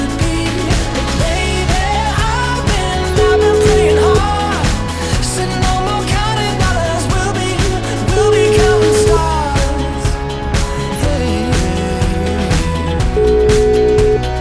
Ringback tone (Europe, 425 Hz), with noise.